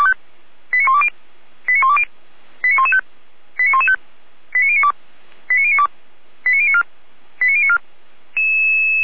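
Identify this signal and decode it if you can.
Five-tone selective call (ZVEI), heard as quick runs of five tones, tones of about 70 ms. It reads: 89120 89120 89115 89115 89092 89092 89094 89094
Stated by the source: Received on an SDR, ZVEI five-tone selective call.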